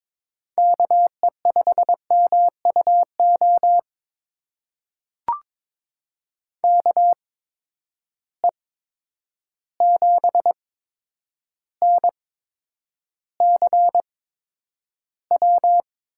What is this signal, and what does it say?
Morse code, heard as a keyed tone, about 21 wpm, spaced as for 9 wpm, 700 Hz. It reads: KE5MUO KE7NCW
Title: USA Call Signs - 6 Characters - 8x Character Spacing - 22wpm